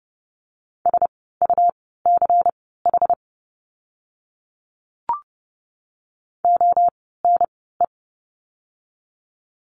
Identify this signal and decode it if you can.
Morse code, heard as a keyed tone, about 30 wpm, spaced as for 17 wpm, 700 Hz. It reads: SUCH ONE
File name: course / 30:10wpm / 237 Lesson 38f - Letter q - Words 30wpm